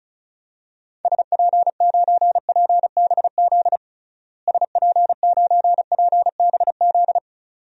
Morse code, heard as a keyed tone, about 36 wpm, 700 Hz. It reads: SP9PBZ SP9PBZ
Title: DXCC-100 Call Signs - 6 Characters - Repeated 2x - 35wpm